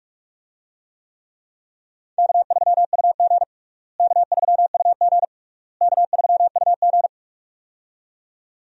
Morse code, heard as a keyed tone, about 46 wpm, 700 Hz. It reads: K3UG K3UG K3UG
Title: USA Call Signs - 4 Characters - Repeated 3x - 45wpm